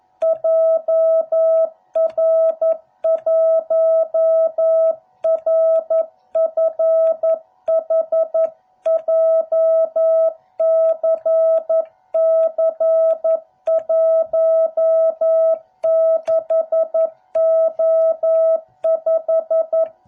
Morse code, heard as a keyed tone, about 11 wpm, 660 Hz. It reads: JR1RFH JCC16O5